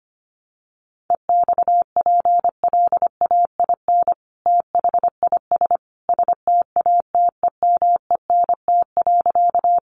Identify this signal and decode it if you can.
Morse code, heard as a keyed tone, about 25 wpm, 700 Hz. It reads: EXPLAIN THIS STATEMENT.
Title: Sentences from Top 1200 Words 25wpm